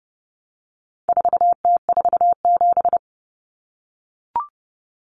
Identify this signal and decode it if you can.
Morse code, heard as a keyed tone, about 30 wpm, 700 Hz. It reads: 4T47
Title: Common Cut Numbers - 4-Digit 30wpm